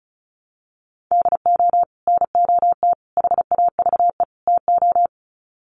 Morse code, heard as a keyed tone, about 34 wpm, 700 Hz. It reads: DO NOT HAVE TO
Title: Sets of 4 Words 35wpm